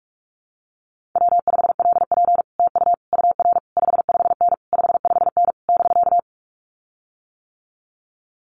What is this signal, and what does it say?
Morse code, heard as a keyed tone, about 44 wpm, 700 Hz. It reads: W5LP TU UR 55N 55N <BK>
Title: Common QSO Phrases Encore 45wpm